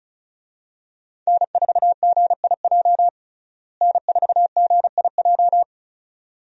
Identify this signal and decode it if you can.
Morse code, heard as a keyed tone, about 36 wpm, 700 Hz. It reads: N4GIJ N4GIJ
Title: USA Call Signs - 5 Characters - Repeated 2x - 35wpm